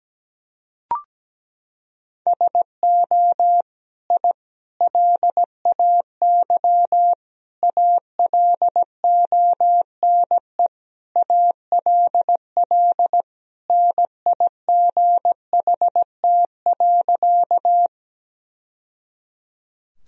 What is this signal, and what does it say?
Morse code, heard as a keyed tone, about 17 wpm, 700 Hz. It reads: SO I LAY ALONE ALL NIGHT.
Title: Sentences from Top 1000 Words - Review Part 2 - 17wpm